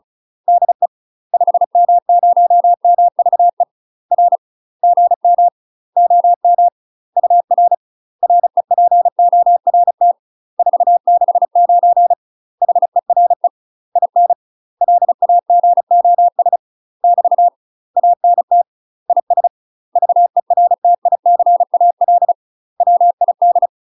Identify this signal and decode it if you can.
Morse code, heard as a keyed tone, about 35 wpm, 700 Hz. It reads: DE 5M0MVE R GM OM UR REPORT 469 HERE IN LAGOS <BT> ANT IS VERTICAL WID
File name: UK MASTER QSOs TRAINING / 35WPM / UK MASTER QSOs 0019